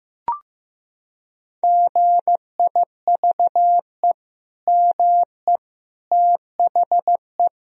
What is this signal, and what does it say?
Morse code, heard as a keyed tone, about 16 wpm, 700 Hz. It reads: GIVE ME THE
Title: Sentences from Top 1000 Words - Review Part 4 - 15wpm